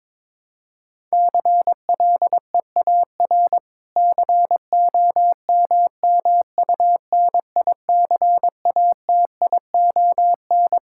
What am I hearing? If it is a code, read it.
Morse code, heard as a keyed tone, about 22 wpm, 700 Hz. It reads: CLEAR COMMUNICATION
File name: Difficult Sentences with Hints - 22wpm